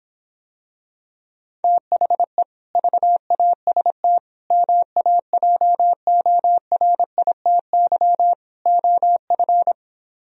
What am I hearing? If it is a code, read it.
Morse code, heard as a keyed tone, about 26 wpm, 700 Hz. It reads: THE VAST MAJORITY OF